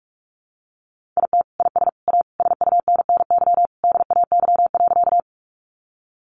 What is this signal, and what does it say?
Morse code, heard as a keyed tone, about 44 wpm, 700 Hz. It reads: IT IS A SUNNY DAY.
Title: Additional Easy Phrases 45wpm